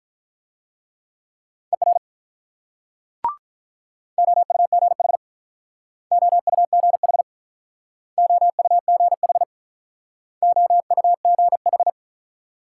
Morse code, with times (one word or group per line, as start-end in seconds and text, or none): 1.7-2.0 ER
4.2-5.2 OUGH
6.1-7.2 OUGH
8.2-9.4 OUGH
10.4-11.9 OUGH